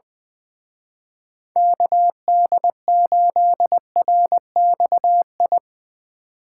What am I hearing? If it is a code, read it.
Morse code, heard as a keyed tone, about 21 wpm, 700 Hz. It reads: KD8RXI
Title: Speed Racing - USA 2x3 Call Signs 20wpm